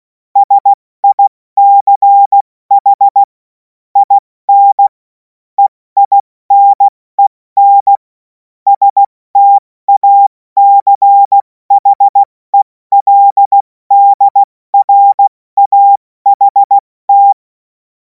Morse code, heard as a keyed tone, about 16 wpm, spaced as for 14 wpm, 800 Hz. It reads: SICH IN EINEN STACHELDRAHT
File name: Morsekurs / Witze / Witz 14. April 2026 70 F80